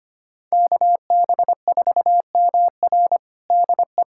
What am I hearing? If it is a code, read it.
Morse code, heard as a keyed tone, about 25 wpm, 700 Hz. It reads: KB4MR DE
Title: USA Callsign DE Callsign - 5 Characters - 25wpm